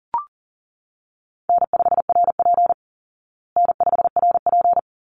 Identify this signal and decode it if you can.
Morse code, heard as a keyed tone, about 40 wpm, 700 Hz. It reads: N5RP N5RP
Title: Easy Call Signs Repeated 2x - 40wpm